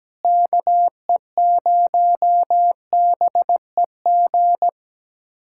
Morse code, heard as a keyed tone, about 17 wpm, 700 Hz. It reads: KE0BEG
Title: USA Callsign DE Callsign 17wpm